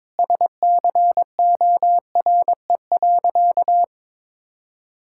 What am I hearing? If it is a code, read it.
Morse code, heard as a keyed tone, about 22 wpm, 700 Hz. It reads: SCORE.
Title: Sentences from Top 1600 Words 22wpm